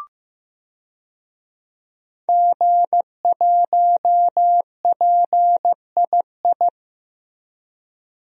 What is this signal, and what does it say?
Morse code, heard as a keyed tone, about 15 wpm, 700 Hz. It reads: G1PII
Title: DXCC-100 Call Signs - 5 Characters - Repeated 3x - 15wpm